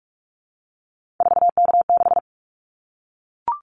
Morse code, heard as a keyed tone, about 44 wpm, 700 Hz. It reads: 4K6
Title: Callsign Prefixes for Top 100 DXCC - 45wpm